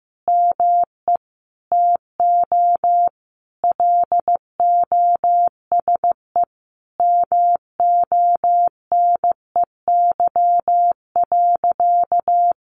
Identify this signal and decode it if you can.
Morse code, heard as a keyed tone, about 15 wpm, 700 Hz. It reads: ME TO LOSE MONEY.
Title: Sentences from Top 1100 Words 15wpm